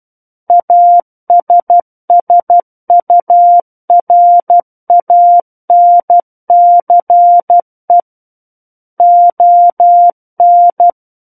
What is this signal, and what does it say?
Morse code, heard as a keyed tone, about 12 wpm, 700 Hz. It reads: ASSURANCE ON
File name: Book One / 03-Chapter 3 - 12 WPM